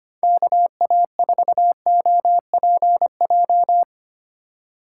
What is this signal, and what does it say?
Morse code, heard as a keyed tone, about 25 wpm, 700 Hz. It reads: KA4OPJ
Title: USA Call Signs - 6 Characters - 3x Character Spacing - 25wpm